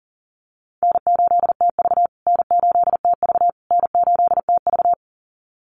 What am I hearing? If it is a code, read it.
Morse code, heard as a keyed tone, about 40 wpm, 700 Hz. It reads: N8TV N8TV N8TV
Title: USA Call Signs Repeated 3x 40wpm